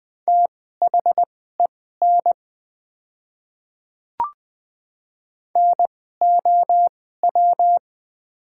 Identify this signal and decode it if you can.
Morse code, heard as a keyed tone, about 20 wpm, spaced as for 14 wpm, 700 Hz. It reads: THEN NOW
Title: 2x Character Spacing - Top 100 Words - 20wpm